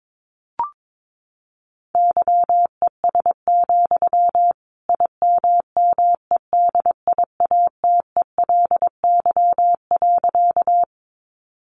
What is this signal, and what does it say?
Morse code, heard as a keyed tone, about 22 wpm, 700 Hz. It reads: YES, IMMEDIATELY.
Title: Sentences from Top 1100 Words 22wpm